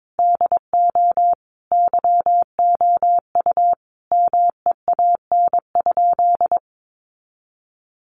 Morse code, heard as a keyed tone, about 22 wpm, 700 Hz. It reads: DO YOU MEAN?
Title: Sentences from Top 200 Words 22wpm